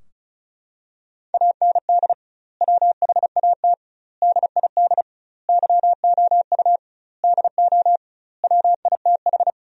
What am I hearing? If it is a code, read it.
Morse code, heard as a keyed tone, about 34 wpm, 700 Hz. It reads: AND WHAT DID YOU DO WITH
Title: 2x Word-Spacing - Sentences from Top 300 Words - 35wpm